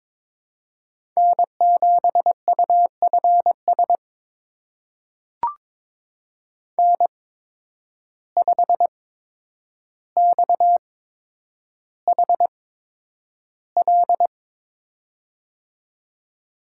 Morse code, heard as a keyed tone, about 22 wpm, spaced as for 9 wpm, 700 Hz. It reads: N7UFS N5XHL
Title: USA Call Signs - 5 Characters - 8x Character Spacing - 22wpm